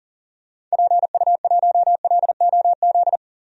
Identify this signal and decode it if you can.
Morse code, heard as a keyed tone, about 40 wpm, 700 Hz. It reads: PU1LOZ
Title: DXCC-100 Call Signs - 6 Characters - 40wpm